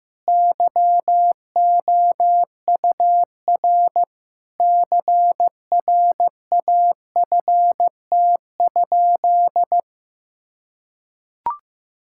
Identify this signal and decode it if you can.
Morse code, heard as a keyed tone, about 15 wpm, 700 Hz. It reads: YOUR CRAFT?
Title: Sentences from Top 1000 Words - Review Part 6 - 15wpm